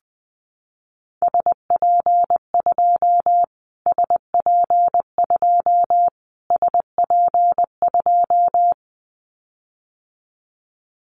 Morse code, heard as a keyed tone, about 20 wpm, 700 Hz. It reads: SP2 SP2 SP2
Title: Callsign Prefixes for Top 25 DXCC - Repeated 3x - 20wpm